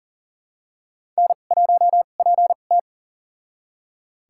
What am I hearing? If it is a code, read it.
Morse code, heard as a keyed tone, about 39 wpm, spaced as for 28 wpm, 700 Hz. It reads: N1PT
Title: USA Call Signs - 4 Characters - 2x Character Spacing - 40wpm